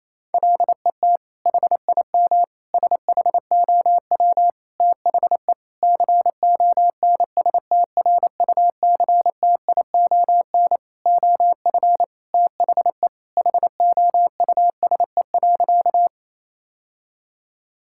Morse code, heard as a keyed tone, about 28 wpm, 700 Hz. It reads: LET HIM SHOW THE CONSTRUCTION OF THE HOUSE.
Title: Sentences from Top 1600 Words 28wpm